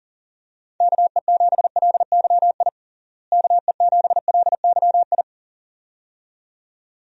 Morse code, heard as a keyed tone, about 40 wpm, 700 Hz. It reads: KE7LYI KE7LYI
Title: USA Call Signs - 6 Characters - Repeated 2x - 40wpm